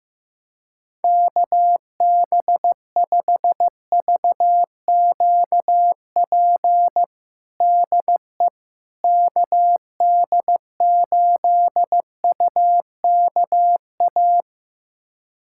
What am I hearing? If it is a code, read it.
Morse code, heard as a keyed tone, about 16 wpm, 700 Hz. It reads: KB5VQP DE KD8UKA
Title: USA Callsign DE Callsign - 6 Characters - 15wpm